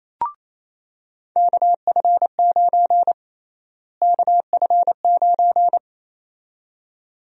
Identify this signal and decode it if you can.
Morse code, heard as a keyed tone, about 28 wpm, 700 Hz. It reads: KF9 KF9
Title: USA Call Sign Prefixes - Repeated 2x - 28wpm